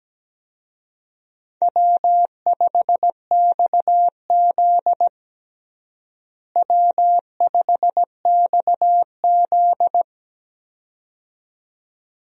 Morse code, heard as a keyed tone, about 17 wpm, 700 Hz. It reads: W5XZ W5XZ
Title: USA Call Signs - 4 Characters - Repeated 2x - 17wpm